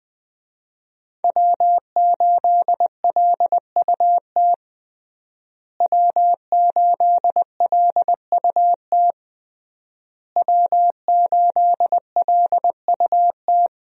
Morse code, heard as a keyed tone, about 20 wpm, 700 Hz. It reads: W8LUT W8LUT W8LUT
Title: USA Call Signs - 5 Characters - Repeated 3x - 20wpm